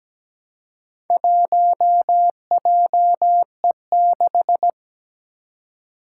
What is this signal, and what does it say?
Morse code, heard as a keyed tone, about 17 wpm, 700 Hz. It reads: 1JE6
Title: Four Challenging Letter-Number Combinations 17wpm